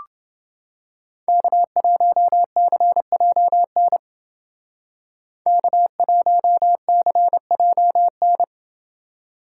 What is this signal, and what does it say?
Morse code, heard as a keyed tone, about 29 wpm, 700 Hz. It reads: K1CJN K1CJN